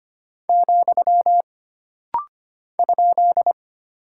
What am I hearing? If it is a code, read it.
Morse code, heard as a keyed tone, about 25 wpm, 700 Hz. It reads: , ?
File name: Single Punctuation-Prosign - Rapid-Fire - 25wpm